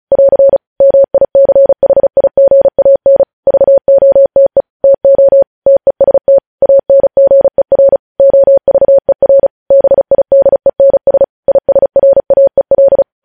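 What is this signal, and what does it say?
Morse code, heard as a keyed tone, about 36 wpm, 550 Hz. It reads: <AR> MICHIGAN VOTE TO TEST ANGER OVER BIDENS ISRAEL
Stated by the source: Morse code news headlines at 35.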